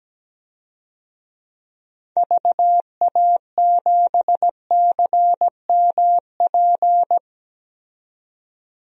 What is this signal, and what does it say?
Morse code, heard as a keyed tone, about 17 wpm, 700 Hz. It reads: VA7CMP
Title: DXCC-100 Call Signs - 6 Characters - 17wpm